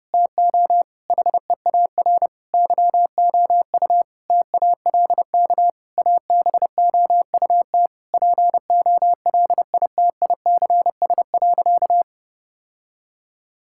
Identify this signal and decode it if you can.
Morse code, heard as a keyed tone, about 32 wpm, 700 Hz. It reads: TO HEAR YOU TALK ABOUT POLITICS.
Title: Sentences from Top 1000 Words - Review Part 6 - 30wpm